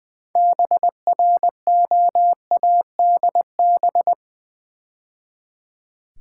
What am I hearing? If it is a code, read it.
Morse code, heard as a keyed tone, about 20 wpm, 700 Hz. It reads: BROADB
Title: Build A Word - 3 Letters At A Time - 9-Letter Words - 20wpm